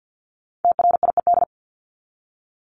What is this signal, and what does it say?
Morse code, heard as a keyed tone, about 50 wpm, 700 Hz. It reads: TRIED
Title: ED Suffix - Top 100 Words with Base Words 50wpm